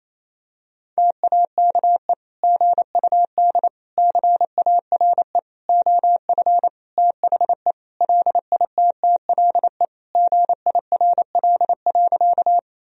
Morse code, heard as a keyed tone, about 28 wpm, 700 Hz. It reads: TAKE GUD CARE OF THE LITTLE GIRL.
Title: Sentences from Top 400 Words and Abbreviations 28wpm